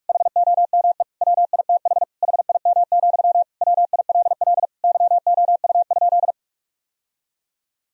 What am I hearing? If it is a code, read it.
Morse code, heard as a keyed tone, about 41 wpm, 700 Hz. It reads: HOME WITH HIM, WILL YOU?
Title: Sentences from Top 1000 Words - Review Part 1 - 45wpm